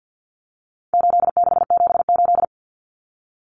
Morse code, heard as a keyed tone, about 50 wpm, 700 Hz. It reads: 8678
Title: Numbers - 4-Digit 50wpm